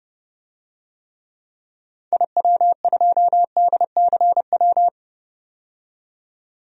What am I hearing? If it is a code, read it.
Morse code, heard as a keyed tone, about 32 wpm, 700 Hz. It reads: IW2DCW
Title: DXCC-100 Call Signs - 6 Characters - 30wpm